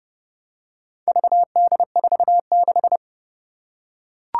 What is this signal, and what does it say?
Morse code, heard as a keyed tone, about 30 wpm, 690 Hz. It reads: VD46
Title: Four Challenging Letter-Number Combinations 30wpm